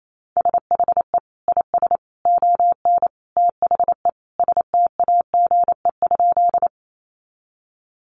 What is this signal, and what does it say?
Morse code, heard as a keyed tone, about 28 wpm, 700 Hz. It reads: SHE IS ON THE STAGE?
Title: Sentences from Top 1200 Words 28wpm